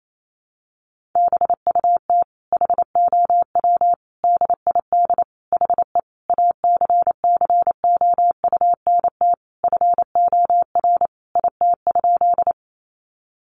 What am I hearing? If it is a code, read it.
Morse code, heard as a keyed tone, about 28 wpm, 700 Hz. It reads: BUT HOW DID HE ACCOUNT FOR IT?
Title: Sentences from Top 900 Words 28wpm